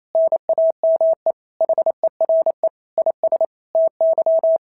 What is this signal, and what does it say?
Morse code, heard as a keyed tone, about 28 wpm, 690 Hz. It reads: NAME HERE IS TY
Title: SOTA QSO Practice 28wpm